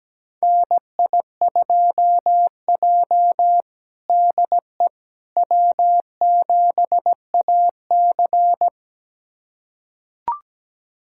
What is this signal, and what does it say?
Morse code, heard as a keyed tone, about 17 wpm, 700 Hz. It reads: NI2J DE W7AC
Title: USA Callsign DE Callsign - 4 Characters - 17wpm